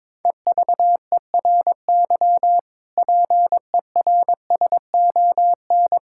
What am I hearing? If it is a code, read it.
Morse code, heard as a keyed tone, about 22 wpm, 700 Hz. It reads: EVERY PERSON
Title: Sentences from Top 1000 Words - Review Part 5 - 22wpm